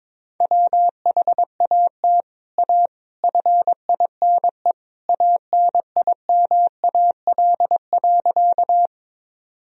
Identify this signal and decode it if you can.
Morse code, heard as a keyed tone, about 22 wpm, 700 Hz. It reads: WHAT A FINE ANIMAL.
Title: Sentences from Top 1000 Words - Review Part 1 - 22wpm